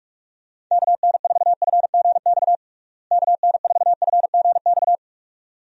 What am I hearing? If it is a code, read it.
Morse code, heard as a keyed tone, about 46 wpm, 700 Hz. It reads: KN4FGX KN4FGX
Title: USA Call Signs - 6 Characters - Repeated 2x - 45wpm